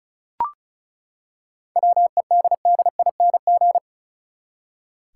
Morse code, heard as a keyed tone, about 36 wpm, 700 Hz. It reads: WEDDING
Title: Seven Letter Words - 35wpm